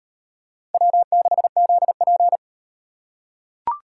Morse code, heard as a keyed tone, about 38 wpm, 700 Hz. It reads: W6ZP